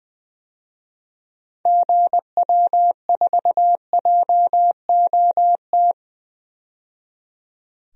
Morse code, heard as a keyed tone, about 20 wpm, 700 Hz. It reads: GW4JOT
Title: DXCC-100 Call Signs - 6 Characters - 20wpm